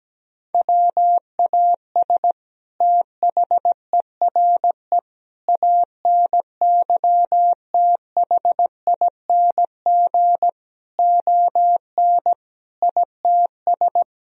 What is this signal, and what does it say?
Morse code, heard as a keyed tone, about 17 wpm, 700 Hz. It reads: WAS THERE ANYTHING ON ITS
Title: Sentences from Top 1000 Words - Review Part 3 - 17wpm